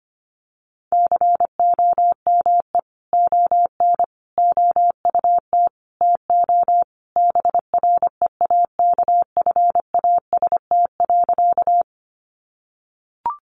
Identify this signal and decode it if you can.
Morse code, heard as a keyed tone, about 25 wpm, 700 Hz. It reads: COME ON OUT TO BREAKFAST.
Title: Sentences from Top 1500 Words 25wpm